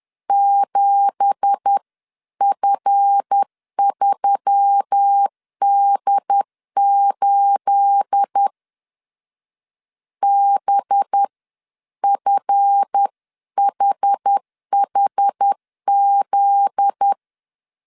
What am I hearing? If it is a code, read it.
Morse code, heard as a keyed tone, about 10 wpm, spaced as for 9 wpm, 800 Hz. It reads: 7F3D8 BFHHZ